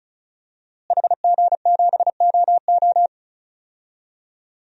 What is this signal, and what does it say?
Morse code, heard as a keyed tone, about 36 wpm, 700 Hz. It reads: HG7OO
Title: DXCC-100 Call Signs - 5 Characters - Repeated 3x - 35wpm